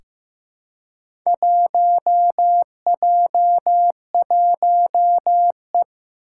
Morse code, heard as a keyed tone, about 15 wpm, 700 Hz. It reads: 1J1E
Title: Four Challenging Letter-Number Combinations 15wpm